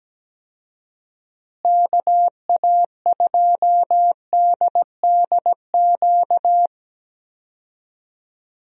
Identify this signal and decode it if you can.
Morse code, heard as a keyed tone, about 17 wpm, 700 Hz. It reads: KA2DDQ
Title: USA Call Signs - 6 Characters - Repeated 3x - 17wpm